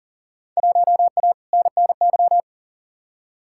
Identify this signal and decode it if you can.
Morse code, heard as a keyed tone, about 40 wpm, 700 Hz. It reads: 1A NNY
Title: Field Day Exchanges - 40wpm